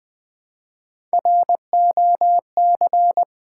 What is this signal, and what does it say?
Morse code, heard as a keyed tone, about 20 wpm, 700 Hz. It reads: ROC
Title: Build A Word - 1 Letter At A Time - 4-Letter Words - 20wpm